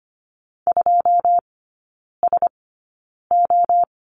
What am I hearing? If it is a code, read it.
Morse code, heard as a keyed tone, about 25 wpm, spaced as for 9 wpm, 690 Hz. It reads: 2SO
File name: Lesson 14e - Number 2 - Three Characters 25wpm